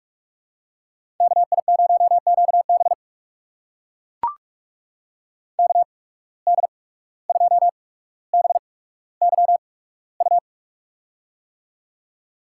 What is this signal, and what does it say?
Morse code, heard as a keyed tone, about 44 wpm, spaced as for 18 wpm, 700 Hz. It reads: KI0QB KD2BYU
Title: USA Call Signs - 8x Character Spacing - 45wpm